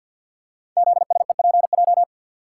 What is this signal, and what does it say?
Morse code, heard as a keyed tone, about 49 wpm, 700 Hz. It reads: ZSEPJ
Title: Code Groups - 5 Letters - 50wpm